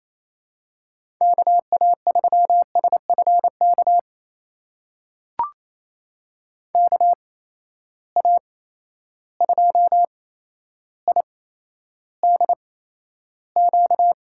Morse code, heard as a keyed tone, about 28 wpm, spaced as for 12 wpm, 700 Hz. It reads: KA3SFK KA2IDQ
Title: USA Call Signs - 6 Characters - 8x Character Spacing - 28wpm